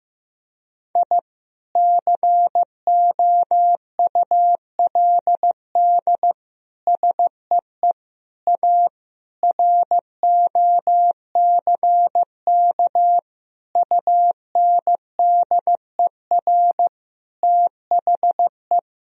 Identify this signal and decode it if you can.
Morse code, heard as a keyed tone, about 16 wpm, 700 Hz. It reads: I COULD SEE A ROCK UNDER THE
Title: Sentences from Top 1000 Words - Review Part 2 - 15wpm